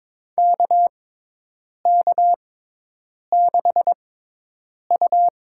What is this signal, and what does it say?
Morse code, heard as a keyed tone, about 21 wpm, spaced as for 8 wpm, 700 Hz. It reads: KK6U
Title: USA Call Signs - 6x Character Spacing - 22wpm